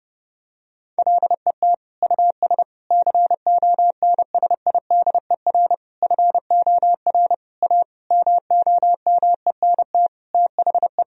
Morse code, heard as a keyed tone, about 32 wpm, 700 Hz. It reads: LET US CONSIDER FOR A MOMENT THE
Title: Sentences from Top 1000 Words - Review Part 4 - 30wpm